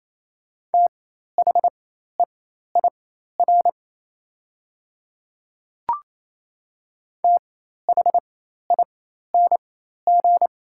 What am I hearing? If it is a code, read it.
Morse code, heard as a keyed tone, about 28 wpm, spaced as for 13 wpm, 700 Hz. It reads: THEIR THING
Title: 4x Character Spacing - Top 100 Words - 28wpm